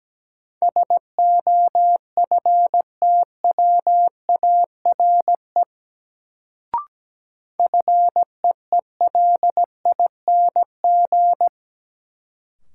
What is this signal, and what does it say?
Morse code, heard as a keyed tone, about 17 wpm, 700 Hz. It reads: SOFTWARE FEELING
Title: Sentences from Top 1000 Words - Review Part 3 - New Word Drill - 17wpm